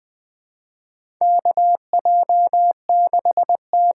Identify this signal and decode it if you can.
Morse code, heard as a keyed tone, about 20 wpm, 700 Hz. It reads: KJ6T
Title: USA Call Signs - 8x Character Spacing - 20wpm